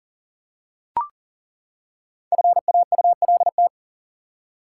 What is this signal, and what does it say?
Morse code, heard as a keyed tone, about 39 wpm, 700 Hz. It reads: FAULT
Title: Five Letter Words - 40wpm